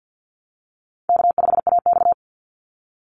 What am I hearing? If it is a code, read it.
Morse code, heard as a keyed tone, about 48 wpm, 700 Hz. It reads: K5AX
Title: Easy Call Signs Repeated 3x - 50wpm